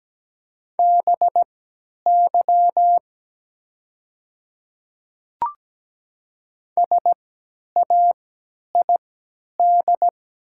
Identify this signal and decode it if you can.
Morse code, heard as a keyed tone, about 17 wpm, spaced as for 10 wpm, 690 Hz. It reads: BY SAID
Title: 3x Character Spacing - Top 100 Words - 17wpm